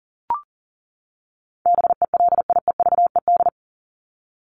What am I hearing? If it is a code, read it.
Morse code, heard as a keyed tone, about 40 wpm, 700 Hz. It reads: BELIEVED
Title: ED Suffix - Top 100 Words with Base Words 40wpm